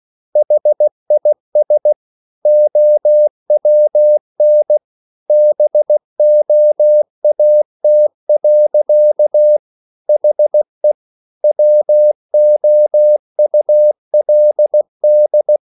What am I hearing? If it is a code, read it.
Morse code, heard as a keyed tone, about 16 wpm, 600 Hz. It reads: HIS OWN BOAT. HE WOULD